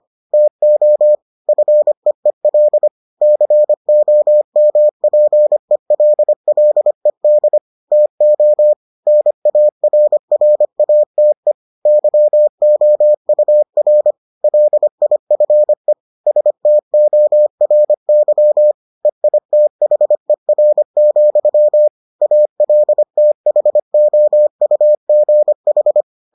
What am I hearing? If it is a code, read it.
Morse code, heard as a keyed tone, about 25 wpm, 600 Hz. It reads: TO FEEL COMPELLED TO NARRATE YOUR LIFE STORY EITHER, ALTHOUGH